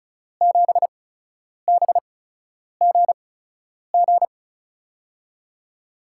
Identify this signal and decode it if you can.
Morse code, heard as a keyed tone, about 34 wpm, spaced as for 10 wpm, 700 Hz. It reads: 7BGG
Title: USA Call Signs - 5 Characters - 8x Character Spacing - 35wpm